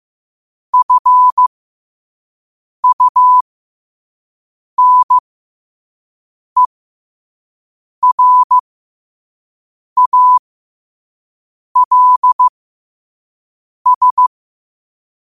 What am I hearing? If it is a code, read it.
Morse code, heard as a keyed tone, about 15 wpm, spaced as for 5 wpm, 1000 Hz. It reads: FUNERALS
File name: qotdfiles / 5wpm